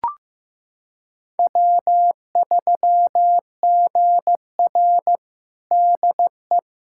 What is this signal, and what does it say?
Morse code, heard as a keyed tone, about 15 wpm, 700 Hz. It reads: W3GR DE
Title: USA Callsign DE Callsign - 4 Characters - 15wpm